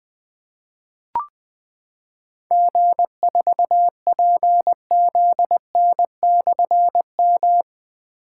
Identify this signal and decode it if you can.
Morse code, heard as a keyed tone, about 20 wpm, 700 Hz. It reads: G4PZN/M
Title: Difficult Call Signs 20wpm